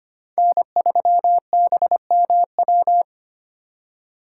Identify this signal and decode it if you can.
Morse code, heard as a keyed tone, about 31 wpm, 700 Hz. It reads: N3BMW
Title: Speed Racing - USA 1x3 Call Signs 25wpm